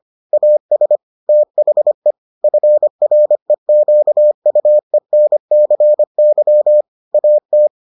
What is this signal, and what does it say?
Morse code, heard as a keyed tone, about 25 wpm, 600 Hz. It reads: AS THE FREQUENCY AT